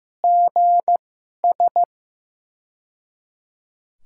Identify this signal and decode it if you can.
Morse code, heard as a keyed tone, about 15 wpm, spaced as for 11 wpm, 700 Hz. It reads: GS
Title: USA Call Signs - 2x Character Spacing - 15wpm